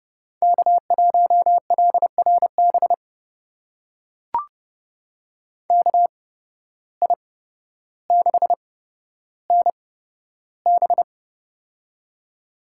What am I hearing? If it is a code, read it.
Morse code, heard as a keyed tone, about 31 wpm, spaced as for 13 wpm, 700 Hz. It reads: K1LRB KI6NB
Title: USA Call Signs - 5 Characters - 8x Character Spacing - 30wpm